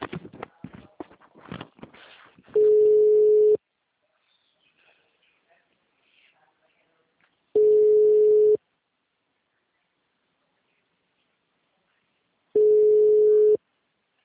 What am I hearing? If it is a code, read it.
Ringback tone (Europe, 425 Hz).